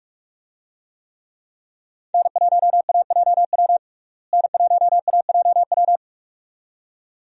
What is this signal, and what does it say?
Morse code, heard as a keyed tone, about 46 wpm, 700 Hz. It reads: N1AJW N1AJW
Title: USA Call Signs - 5 Characters - Repeated 2x - 45wpm